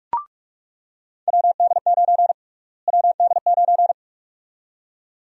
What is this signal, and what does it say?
Morse code, heard as a keyed tone, about 46 wpm, 700 Hz. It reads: WD9 WD9
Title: USA Call Sign Prefixes - Repeated 2x - 45wpm